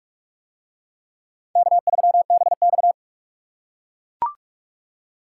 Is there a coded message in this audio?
Yes: Morse code from 1.6 to 2.9 s, heard as a keyed tone, about 44 wpm, 700 Hz. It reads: K3BX